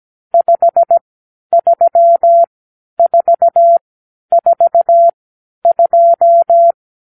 Morse code, heard as a keyed tone, about 17 wpm, spaced as for 10 wpm, 680 Hz. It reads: 53442
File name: arrl / Review (1-5)